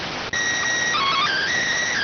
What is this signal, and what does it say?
SSTV (slow-scan television), PD 290.